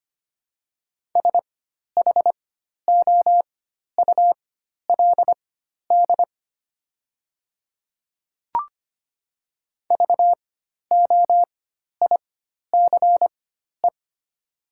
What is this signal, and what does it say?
Morse code, heard as a keyed tone, about 25 wpm, spaced as for 12 wpm, 700 Hz. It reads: SHOULD VOICE
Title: 4x Character Spacing - Top 500 Words - 25wpm